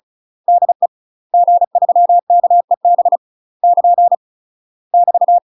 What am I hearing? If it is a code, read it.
Morse code, heard as a keyed tone, about 36 wpm, 700 Hz. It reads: DE G3KEB <KN> <BT>
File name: UK MASTER QSOs TRAINING / 35WPM / UK MASTER QSOs 0009